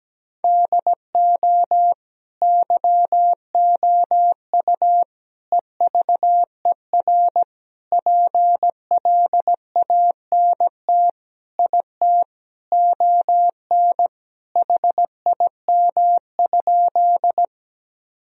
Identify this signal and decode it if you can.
Morse code, heard as a keyed tone, about 17 wpm, 700 Hz. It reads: DO YOU EVER PLANT IT ON HIM?
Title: Sentences from Top 1000 Words - Review Part 1 - 17wpm